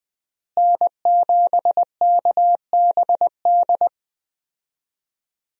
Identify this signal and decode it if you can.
Morse code, heard as a keyed tone, about 20 wpm, 700 Hz. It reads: N7KBD
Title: USA Call Signs - 5 Characters - Repeated 3x - 20wpm